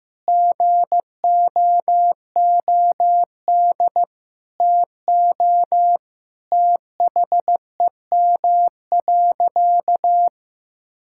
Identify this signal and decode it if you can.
Morse code, heard as a keyed tone, about 15 wpm, 700 Hz. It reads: GOOD TO THEM.
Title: Sentences from Top 1000 Words - Review Part 6 - 15wpm